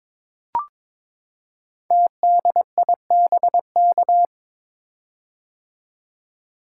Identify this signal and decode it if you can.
Morse code, heard as a keyed tone, about 23 wpm, 700 Hz. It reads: TDIBK
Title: Code Groups - 5 Letters - 22wpm